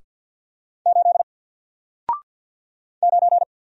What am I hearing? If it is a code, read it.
Morse code, heard as a keyed tone, about 48 wpm, 700 Hz. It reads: 8 9
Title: Single Character (Letter:Number:Punctuation:Prosign) 50wpm